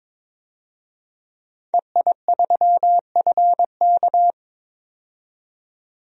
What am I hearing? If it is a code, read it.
Morse code, heard as a keyed tone, about 22 wpm, 700 Hz. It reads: EI3FK
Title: DXCC-100 Call Signs - 5 Characters - 22wpm